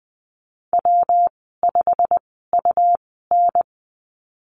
Morse code, heard as a keyed tone, about 20 wpm, spaced as for 15 wpm, 700 Hz. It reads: W5UN
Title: 128 Lesson 21g - Number 6 - Call Signs 20wpm